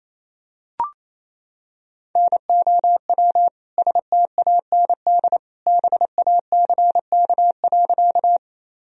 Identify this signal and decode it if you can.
Morse code, heard as a keyed tone, about 28 wpm, 700 Hz. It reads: NOW STAND BACK.
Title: Sentences from Top 1000 Words - Review Part 1 - 28wpm